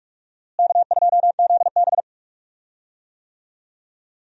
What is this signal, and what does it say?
Morse code, heard as a keyed tone, about 46 wpm, 700 Hz. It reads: K2ZB
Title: USA Call Signs - 4 Characters - Repeated 3x - 45wpm